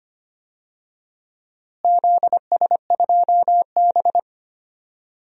Morse code, heard as a keyed tone, about 25 wpm, 700 Hz. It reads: ZS2B
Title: DXCC-100 Call Signs - 4 Characters - Repeated 3x - 25wpm